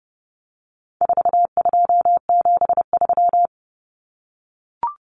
Morse code, heard as a keyed tone, about 29 wpm, 700 Hz. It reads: 4273
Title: Numbers - 4-Digit 30wpm